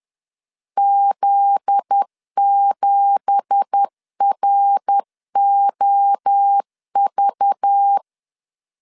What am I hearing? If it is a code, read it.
Morse code, heard as a keyed tone, about 10 wpm, 800 Hz. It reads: Z7ROV
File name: LEZIONI / 9 2867 10wpm 2733